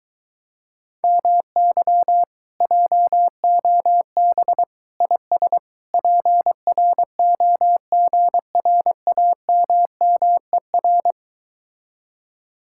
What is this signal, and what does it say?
Morse code, heard as a keyed tone, about 22 wpm, 700 Hz. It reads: MY JOB IS PROGRAMMER